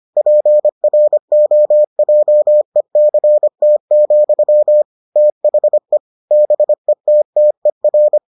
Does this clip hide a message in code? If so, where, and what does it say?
Yes: Morse code from 0.2 to 8.2 s, heard as a keyed tone, about 25 wpm, 600 Hz. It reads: PROJECT, THE BETTER